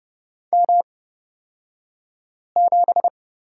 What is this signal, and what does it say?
Morse code, heard as a keyed tone, about 30 wpm, spaced as for 5 wpm, 700 Hz. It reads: M7
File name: Lesson 37d - Letter x - Two Characters 30wpm